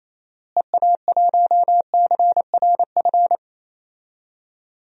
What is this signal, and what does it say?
Morse code, heard as a keyed tone, about 28 wpm, 700 Hz. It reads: EA1CRF
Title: DXCC-100 Call Signs - 6 Characters - Repeated 2x - 28wpm